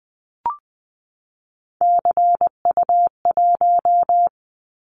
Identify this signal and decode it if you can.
Morse code, heard as a keyed tone, about 20 wpm, 700 Hz. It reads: CU1
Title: Callsign Prefixes for Top 50 DXCC - 20wpm